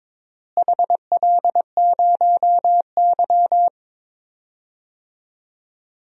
Morse code, heard as a keyed tone, about 22 wpm, 700 Hz. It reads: HL0Y
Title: DXCC-100 Call Signs - 4 Characters - Repeated 2x - 22wpm